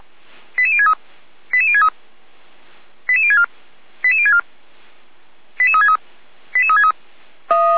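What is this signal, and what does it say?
Five-tone selective call (ZVEI), heard as quick runs of five tones, tones of about 70 ms. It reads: 88063 88063 88064 88064 88363 88363
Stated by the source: ZVEI five-tone selective call 88365 received by SDR.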